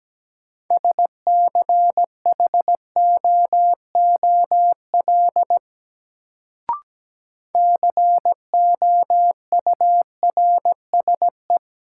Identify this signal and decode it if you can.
Morse code, heard as a keyed tone, about 17 wpm, 700 Hz. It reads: SCHOOL COURSE
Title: Sets of 2 Words - New Word Drill - 17wpm